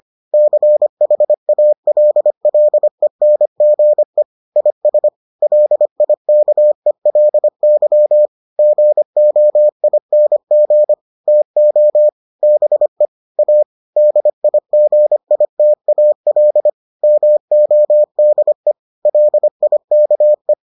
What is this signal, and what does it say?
Morse code, heard as a keyed tone, about 25 wpm, 600 Hz. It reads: CHALLENGE IS LIKELY GOING TO BE A DIGITAL MODE LIKE